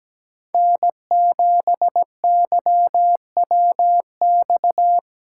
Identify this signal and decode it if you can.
Morse code, heard as a keyed tone, about 20 wpm, 700 Hz. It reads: N7YWX
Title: Speed Racing - USA 1x3 Call Signs 17wpm